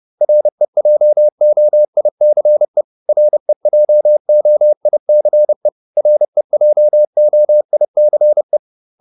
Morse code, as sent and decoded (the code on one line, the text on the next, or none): .-. .--- --- .. -.-. / .-. .--- --- .. -.-. / .-. .--- --- .. -.-. .
REJOICE REJOICE REJOICE